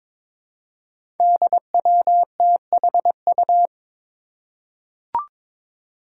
Morse code, heard as a keyed tone, about 23 wpm, 700 Hz. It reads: DWTHU
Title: Five Challenging Letter-Number Combinations 22wpm